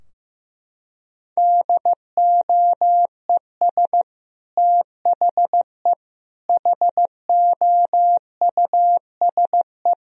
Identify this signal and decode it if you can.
Morse code, heard as a keyed tone, about 15 wpm, 700 Hz. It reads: DOES THE HOUSE
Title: Sentences from Top 1000 Words - Review Part 2 - 15wpm